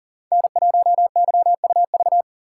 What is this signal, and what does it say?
Morse code, heard as a keyed tone, about 40 wpm, 700 Hz. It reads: N1YUV
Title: USA Call Signs - 5 Characters - 3x Character Spacing - 40wpm